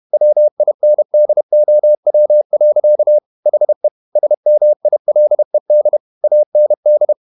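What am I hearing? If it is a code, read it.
Morse code, heard as a keyed tone, about 31 wpm, 600 Hz. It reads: WINDOW. HE SMILED AND
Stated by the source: Morse code at 30 wpm, the novel Never by Ken Follett in English, from CW med Gnister.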